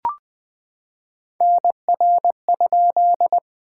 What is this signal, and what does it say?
Morse code, heard as a keyed tone, about 21 wpm, 700 Hz. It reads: NR?
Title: Common QSO Phrases - 2x Word-Spacing - 20wpm